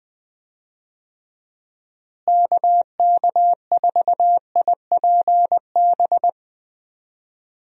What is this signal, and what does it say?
Morse code, heard as a keyed tone, about 20 wpm, 700 Hz. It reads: KK4IPB
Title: USA Call Signs - 6 Characters - Repeated 2x - 20wpm